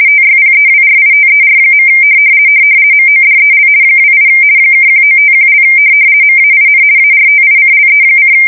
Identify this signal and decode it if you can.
RTTY, about 50 baud. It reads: TYPEWRITER. MODEL 28 (1951) MAIN ARTICLE:
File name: fsk170-50B / teletype-corp